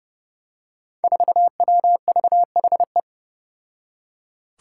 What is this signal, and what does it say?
Morse code, heard as a keyed tone, about 30 wpm, 700 Hz. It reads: 4WVHE
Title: Five Challenging Letter-Number Combinations 30wpm